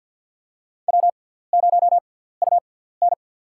Morse code, heard as a keyed tone, about 48 wpm, spaced as for 17 wpm, 700 Hz. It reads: W0UN
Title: USA Call Signs - 4 Characters - 6x Character Spacing - 50wpm